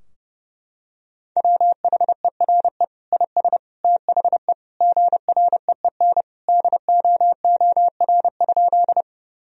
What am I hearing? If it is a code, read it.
Morse code, heard as a keyed tone, about 30 wpm, 700 Hz. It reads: WHERE IS THE GREEN DOOR?
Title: Sentences from Top 1000 Words - Review Part 2 - 30wpm